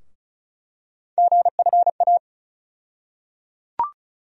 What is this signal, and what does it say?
Morse code, heard as a keyed tone, about 34 wpm, 700 Hz. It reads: GFA
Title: USA Call Sign Postfixes - 3 Letters 35wpm